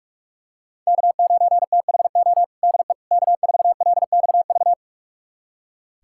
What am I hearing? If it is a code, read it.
Morse code, heard as a keyed tone, about 46 wpm, 700 Hz. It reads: K9THO DE K4LXV
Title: USA Callsign DE Callsign 45wpm